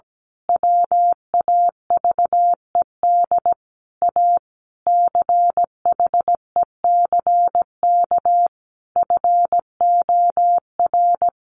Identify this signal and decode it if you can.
Morse code, heard as a keyed tone, about 17 wpm, 700 Hz. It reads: WAVED A CHECK FOR
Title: A Sound Of Thunder 17wpm